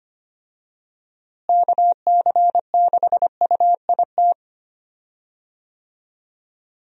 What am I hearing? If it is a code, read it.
Morse code, heard as a keyed tone, about 25 wpm, 700 Hz. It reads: KC6UIT
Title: USA Call Signs - 6 Characters - Repeated 2x - 25wpm